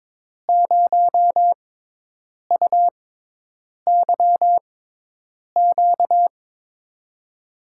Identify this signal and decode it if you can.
Morse code, heard as a keyed tone, about 23 wpm, spaced as for 8 wpm, 700 Hz. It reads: 0UYQ
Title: USA Call Signs - 5 Characters - 6x Character Spacing - 22wpm